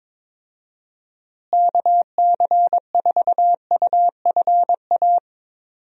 Morse code, heard as a keyed tone, about 21 wpm, 700 Hz. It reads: KC4UFA
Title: USA Call Signs - 6 Characters - Repeated 2x - 22wpm